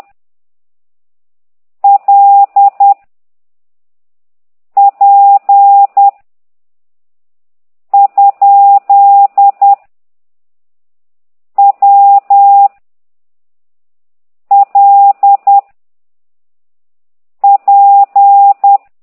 Morse code, heard as a keyed tone, about 10 wpm, 800 Hz. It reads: L P ? W L P